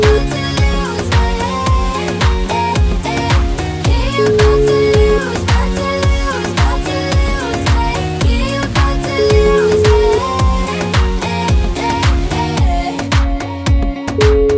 Ringback tone (Europe, 425 Hz), with noise.